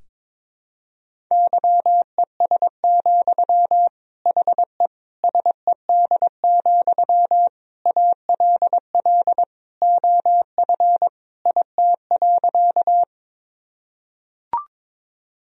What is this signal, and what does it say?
Morse code, heard as a keyed tone, about 22 wpm, 700 Hz. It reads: YES, HE SED, ALL OF IT.
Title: Sentences from Top 500 Words and Abbreviations 22wpm